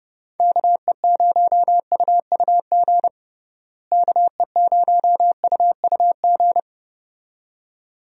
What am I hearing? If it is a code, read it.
Morse code, heard as a keyed tone, about 32 wpm, 700 Hz. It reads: KE0UUG KE0UUG
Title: USA Call Signs - 6 Characters - Repeated 2x - 30wpm